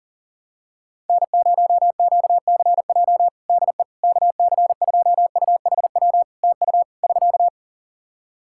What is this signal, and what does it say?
Morse code, heard as a keyed tone, about 40 wpm, 700 Hz. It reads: N0QCJ DE KC2UHW TU <SK>
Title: K1USN SST QSO Practice 40wpm